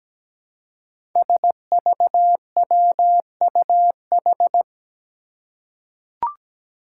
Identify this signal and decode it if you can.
Morse code, heard as a keyed tone, about 17 wpm, 700 Hz. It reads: SVWUH
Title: Five Challenging Letter-Number Combinations 17wpm